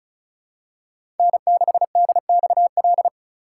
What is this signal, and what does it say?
Morse code, heard as a keyed tone, about 36 wpm, 700 Hz. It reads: N6DXL
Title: USA Call Signs - 5 Characters - 3x Character Spacing - 35wpm